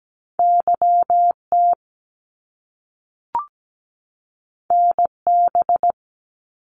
Morse code, heard as a keyed tone, about 17 wpm, 700 Hz. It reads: YT NB
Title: Canadian Provinces and Territories 17wpm